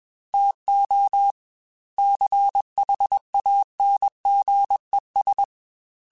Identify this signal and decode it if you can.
Morse code, heard as a keyed tone, about 21 wpm, 800 Hz. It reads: TO CHANGES
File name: Technical Subjects / Technical Subjects-mp3s / Reactance @20